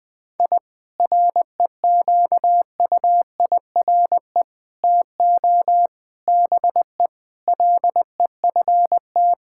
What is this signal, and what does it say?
Morse code, heard as a keyed tone, about 20 wpm, 700 Hz. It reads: I REQUIRE TO BE LEFT
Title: Sentences from Top 1000 Words - Review Part 6 - 20wpm